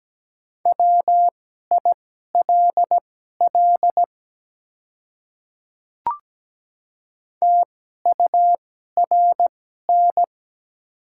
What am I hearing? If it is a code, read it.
Morse code, heard as a keyed tone, about 17 wpm, spaced as for 12 wpm, 700 Hz. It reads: WILL TURN
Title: 2x Character Spacing - Top 200 Words - 17wpm